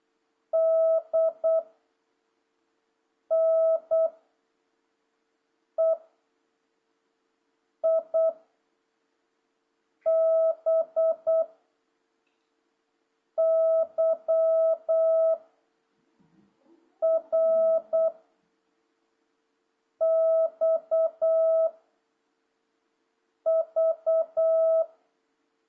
Morse code, heard as a keyed tone, about 8 wpm, 650 Hz. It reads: D N E I B Y R X V
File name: jh8cbh / 2026cw